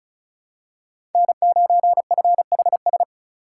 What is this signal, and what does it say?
Morse code, heard as a keyed tone, about 34 wpm, 700 Hz. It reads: N9FHS
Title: USA Call Signs - 5 Characters - 3x Character Spacing - 35wpm